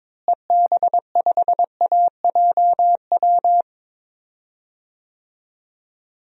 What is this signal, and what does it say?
Morse code, heard as a keyed tone, about 22 wpm, 700 Hz. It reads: EB5AJW
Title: DXCC-100 Call Signs - 6 Characters - Repeated 3x - 22wpm